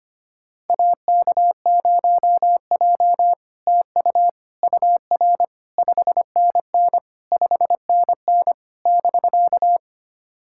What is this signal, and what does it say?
Morse code, heard as a keyed tone, about 25 wpm, 690 Hz. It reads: AK0J TU UR 5NN 5NN <BK>
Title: SOTA QSO Practice 25wpm